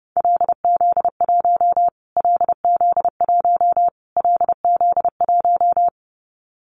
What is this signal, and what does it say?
Morse code, heard as a keyed tone, about 28 wpm, 700 Hz. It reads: LZ1 LZ1 LZ1
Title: Callsign Prefixes for Top 25 DXCC - Repeated 3x - 30wpm